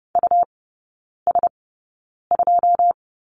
Morse code, heard as a keyed tone, about 30 wpm, spaced as for 9 wpm, 700 Hz. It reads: US2
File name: Lesson 19e - Letter w - Three Characters 30wpm